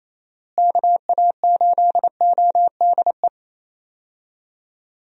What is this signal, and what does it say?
Morse code, heard as a keyed tone, about 28 wpm, 700 Hz. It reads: KA8ODE
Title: USA Call Signs - 6 Characters - Repeated 2x - 28wpm